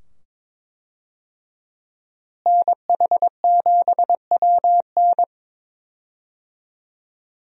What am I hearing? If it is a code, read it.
Morse code, heard as a keyed tone, about 21 wpm, 700 Hz. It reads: NH7WN
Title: USA Call Signs - 5 Characters - Repeated 2x - 22wpm